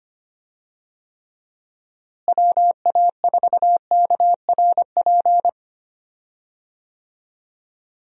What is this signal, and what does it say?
Morse code, heard as a keyed tone, about 25 wpm, 700 Hz. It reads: WA4KRP
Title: USA Call Signs - 6 Characters - Repeated 3x - 25wpm